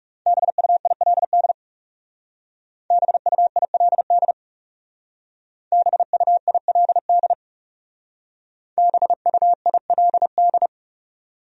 Morse code, heard as a keyed tone, about 38 wpm, 700 Hz. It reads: BUILD BUILD BUILD BUILD